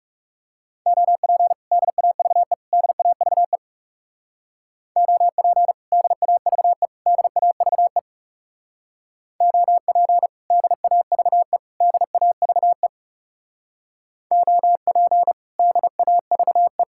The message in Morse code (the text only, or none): OP DAVE DAVE OP DAVE DAVE OP DAVE DAVE OP DAVE